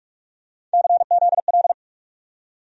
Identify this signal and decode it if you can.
Morse code, heard as a keyed tone, about 42 wpm, 700 Hz. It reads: CZL
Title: USA Call Sign Postfixes - 3 Letters 45wpm